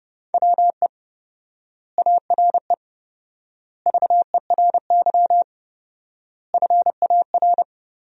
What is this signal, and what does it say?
Morse code, heard as a keyed tone, about 30 wpm, 700 Hz. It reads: WE ARE VERY FAR
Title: 4x Word-Spacing - Sentences from Top 300 Words - 30wpm